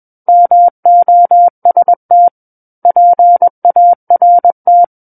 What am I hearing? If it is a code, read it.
Morse code, heard as a keyed tone, about 21 wpm, 700 Hz. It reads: MOST PART
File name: Book One / 12-Chapter 12 - 21 WPM